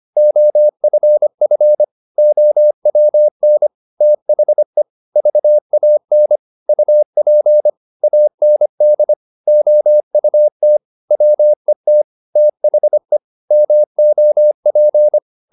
Morse code, heard as a keyed tone, about 25 wpm, 600 Hz. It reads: OFF OWN THE VAN UP AND OUT WET THE MOP